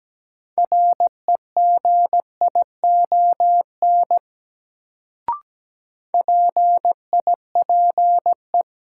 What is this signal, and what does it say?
Morse code, heard as a keyed tone, about 17 wpm, 700 Hz. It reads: REGION PIPE
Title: Sentences from Top 1500 Words - New Word Drill - 17wpm